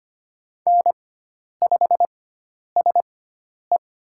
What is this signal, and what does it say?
Morse code, heard as a keyed tone, about 25 wpm, spaced as for 10 wpm, 700 Hz. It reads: N5SE
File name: Lesson 16g - Letter c - Call Signs 25wpm